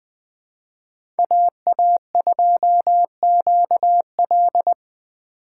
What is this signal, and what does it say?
Morse code, heard as a keyed tone, about 20 wpm, 700 Hz. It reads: AA2QL
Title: USA Call Signs - 5 Characters - 4x Character Spacing - 20wpm